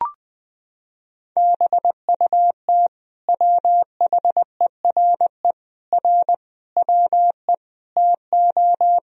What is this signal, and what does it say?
Morse code, heard as a keyed tone, about 21 wpm, 700 Hz. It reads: BUT WHERE R WE TO
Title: Sentences from Top 200 Words and Abbreviations 20wpm